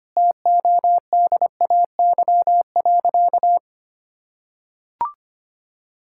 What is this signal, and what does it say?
Morse code, heard as a keyed tone, about 25 wpm, 700 Hz. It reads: TODAY.
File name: practice / 25wpm / Sentences with 9-Word Phrases - 25wpm